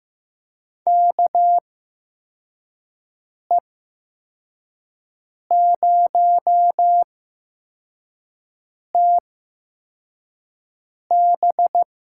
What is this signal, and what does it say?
Morse code, heard as a keyed tone, about 15 wpm, spaced as for 4 wpm, 700 Hz. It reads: KE0TB
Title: USA Call Signs - 8x Character Spacing - 15wpm